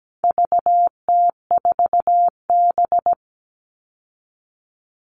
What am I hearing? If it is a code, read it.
Morse code, heard as a keyed tone, about 17 wpm, 700 Hz. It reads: VT4B
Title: All Cut Numbers - 4-Digit 17wpm